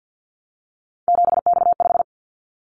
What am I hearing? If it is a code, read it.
Morse code, heard as a keyed tone, about 50 wpm, 700 Hz. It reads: 7X5
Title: Callsign Prefixes for Top 100 DXCC - 50wpm